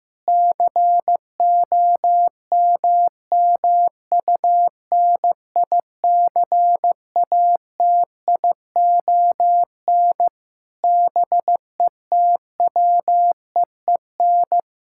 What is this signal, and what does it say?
Morse code, heard as a keyed tone, about 16 wpm, 700 Hz. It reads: COMMUNICATION BETWEEN
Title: Sentences from Top 1000 Words - Review Part 6 - 15wpm